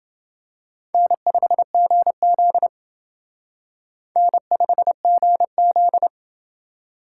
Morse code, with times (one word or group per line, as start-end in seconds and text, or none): 0.9-2.7 N5GZ
4.2-6.1 N5GZ